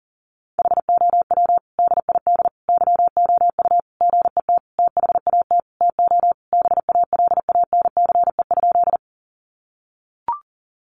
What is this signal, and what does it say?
Morse code, heard as a keyed tone, about 40 wpm, 700 Hz. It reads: HOW DID YOU GET THAT TO BALANCE?
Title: Sentences from Top 900 Words 40wpm